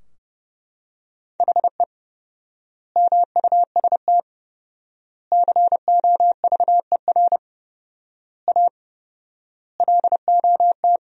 Morse code, heard as a keyed tone, about 30 wpm, 700 Hz. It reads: HE MUST COVER A LOT
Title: 4x Word-Spacing - Sentences from Top 500 Words - 30wpm